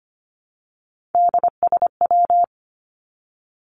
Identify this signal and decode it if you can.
Morse code, heard as a keyed tone, about 25 wpm, 700 Hz. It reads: DSW
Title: Abbreviations Repeated 3x - 25wpm